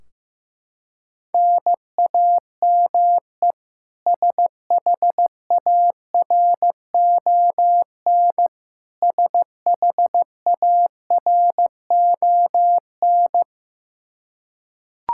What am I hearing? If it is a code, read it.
Morse code, heard as a keyed tone, about 15 wpm, 700 Hz. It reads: NAME SHARON SHARON
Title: Practice QSO - Exchange 1 - Beginning - 15wpm